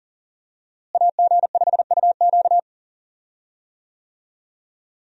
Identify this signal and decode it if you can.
Morse code, heard as a keyed tone, about 40 wpm, 700 Hz. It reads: AG5UQ
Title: USA Call Signs - 5 Characters - Repeated 3x - 40wpm